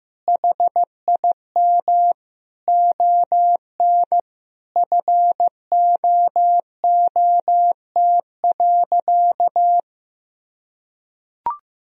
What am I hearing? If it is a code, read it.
Morse code, heard as a keyed tone, about 15 wpm, 700 Hz. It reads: HIM ON FOOT.
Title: Sentences from Top 1000 Words - Review Part 3 - 15wpm